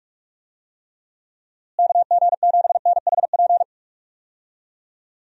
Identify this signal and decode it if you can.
Morse code, heard as a keyed tone, about 46 wpm, 700 Hz. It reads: KG7NHP
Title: USA Call Signs - 6 Characters - Repeated 3x - 45wpm